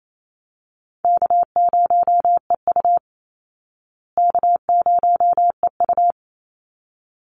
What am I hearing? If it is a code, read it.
Morse code, heard as a keyed tone, about 28 wpm, 700 Hz. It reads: K0EU K0EU
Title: Easy Call Signs Repeated 2x - 28wpm